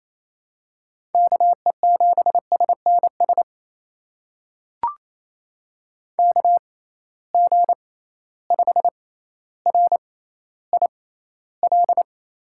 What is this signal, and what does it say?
Morse code, heard as a keyed tone, about 28 wpm, spaced as for 14 wpm, 700 Hz. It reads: KE7SNS KG5RIL
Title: USA Call Signs - 6 Characters - 6x Character Spacing - 28wpm